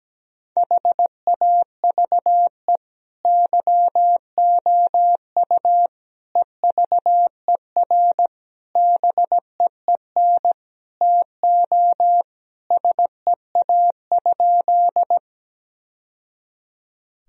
Morse code, heard as a keyed tone, about 17 wpm, 700 Hz. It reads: HAVE YOU EVER BEEN TO SEA?
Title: Sentences from Top 1000 Words - Review Part 1 - 17wpm